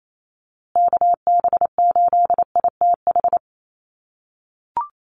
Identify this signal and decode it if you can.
Morse code, heard as a keyed tone, about 28 wpm, 700 Hz. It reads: KB8ITH
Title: USA Call Signs Repeated 2x 28wpm